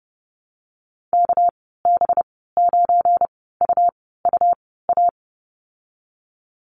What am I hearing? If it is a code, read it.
Morse code, heard as a keyed tone, about 31 wpm, spaced as for 17 wpm, 700 Hz. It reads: KB9UUA
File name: Lesson 37g - Letter x - Call Signs 30wpm